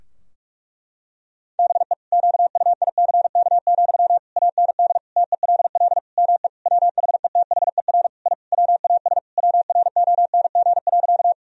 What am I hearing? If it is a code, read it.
Morse code, heard as a keyed tone, about 46 wpm, 700 Hz. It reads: BE QUICK, AND TELL ME WHETHER I WAS WRONG.
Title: Sentences from Top 1000 Words - Review Part 3 - 45wpm